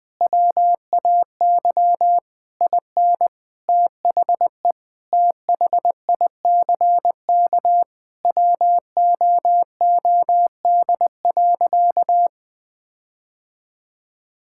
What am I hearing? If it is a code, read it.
Morse code, heard as a keyed tone, about 20 wpm, 700 Hz. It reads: WAY IN THE THICK WOOD.
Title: Sentences from Top 1600 Words 20wpm